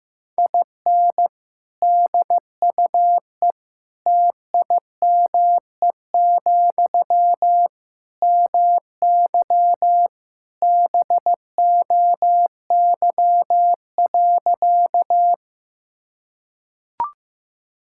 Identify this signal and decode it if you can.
Morse code, heard as a keyed tone, about 15 wpm, 700 Hz. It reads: IN DUE TIME, MY BOY.
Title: Sentences from Top 1000 Words - Review Part 6 - 15wpm